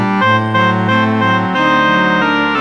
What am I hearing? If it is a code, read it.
There is no code.